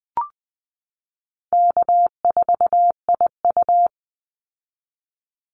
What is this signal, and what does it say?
Morse code, heard as a keyed tone, about 19 wpm, 700 Hz. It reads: K4IU
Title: Easy Call Signs Repeated 3x - 20wpm